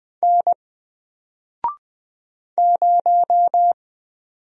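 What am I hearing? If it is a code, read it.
Morse code, heard as a keyed tone, about 20 wpm, 700 Hz. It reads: N 0
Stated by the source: Single letter and number Morse drill at 20 wpm, from N4LCD.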